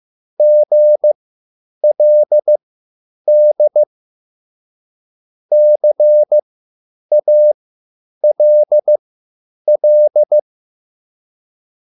Morse code, heard as a keyed tone, about 15 wpm, spaced as for 9 wpm, 600 Hz. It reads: GLD CALL
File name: QSL / CW lesson